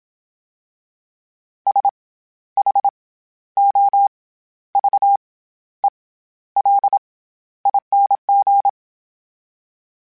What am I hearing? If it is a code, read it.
Morse code, heard as a keyed tone, about 27 wpm, 800 Hz. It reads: S H O V E L ING
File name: Special INGs-mp3s / Special INGs-mp3s / SlowToFaster ing @25